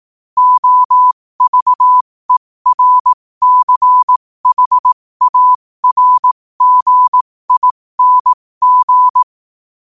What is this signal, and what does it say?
Morse code, heard as a keyed tone, about 18 wpm, spaced as for 15 wpm, 1000 Hz. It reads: OVERCHARGING
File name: qotdfiles / 15wpm